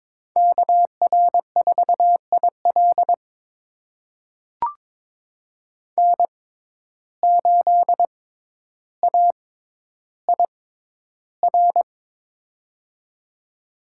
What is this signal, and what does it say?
Morse code, heard as a keyed tone, about 22 wpm, spaced as for 11 wpm, 700 Hz. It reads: KR4IL N8AIR
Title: USA Call Signs - 5 Characters - 6x Character Spacing - 22wpm